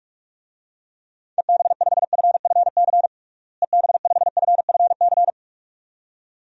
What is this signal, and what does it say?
Morse code, heard as a keyed tone, about 47 wpm, 700 Hz. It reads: EB5FFC EB5FFC
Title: DXCC-100 Call Signs - 6 Characters - Repeated 2x - 45wpm